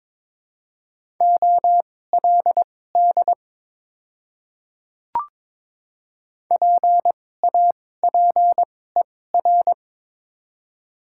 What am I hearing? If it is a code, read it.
Morse code, heard as a keyed tone, about 22 wpm, spaced as for 16 wpm, 700 Hz. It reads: OLD PAPER
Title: 2x Character Spacing - Top 300 Words - 22wpm